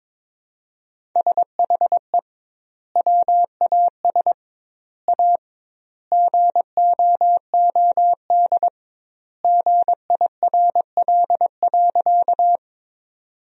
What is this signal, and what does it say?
Morse code, heard as a keyed tone, about 22 wpm, 700 Hz. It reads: SHE WAS A GOOD GIRL.
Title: 2x Word-Spacing - Sentences from Top 400 Words - 22wpm